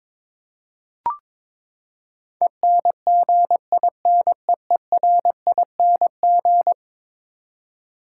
Morse code, heard as a keyed tone, about 22 wpm, 700 Hz. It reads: ENGINEERING
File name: practice / 22wpm / Eleven Letter Words - 22wpm